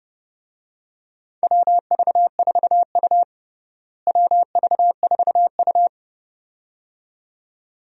Morse code, heard as a keyed tone, about 30 wpm, 700 Hz. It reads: WV4U WV4U
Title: USA Call Signs - 4 Characters - Repeated 2x - 30wpm